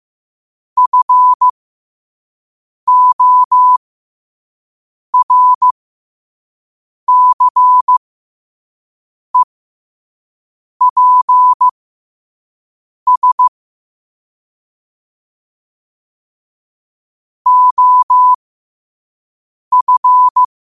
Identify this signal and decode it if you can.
Morse code, heard as a keyed tone, about 15 wpm, spaced as for 5 wpm, 1000 Hz. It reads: FORCEPS OF
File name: qotdfiles / 5wpm